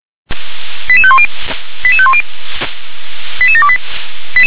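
Five-tone selective call (ZVEI), heard as quick runs of five tones, tones of about 70 ms. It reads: 88510 88510 88728
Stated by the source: ZVEI five-tone selective call received on SDR.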